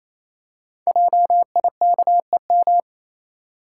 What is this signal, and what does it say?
Morse code, heard as a keyed tone, about 29 wpm, 700 Hz. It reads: JIKEM
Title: Code Groups - 5 Letters - 28wpm